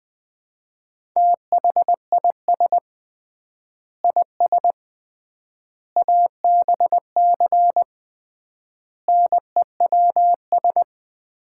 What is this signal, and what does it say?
Morse code, heard as a keyed tone, about 20 wpm, 700 Hz. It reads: THIS IS ABC NEWS
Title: 3x Word-Spacing - Sets of 5 Words - 20wpm